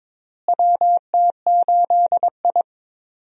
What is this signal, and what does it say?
Morse code, heard as a keyed tone, about 21 wpm, 700 Hz. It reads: WT8I
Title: USA Call Signs - 4 Characters - Repeated 2x - 22wpm